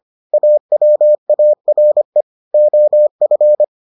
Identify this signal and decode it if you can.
Morse code, heard as a keyed tone, about 25 wpm, 600 Hz. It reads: AWARE OF